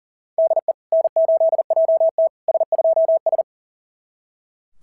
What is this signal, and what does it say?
Morse code, heard as a keyed tone, about 40 wpm, 660 Hz. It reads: DE N8JT S2S
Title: SOTA QSO Practice 40wpm